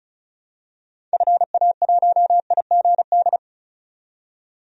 Morse code, heard as a keyed tone, about 34 wpm, 700 Hz. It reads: FA1IGD
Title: DXCC-100 Call Signs - 6 Characters - 35wpm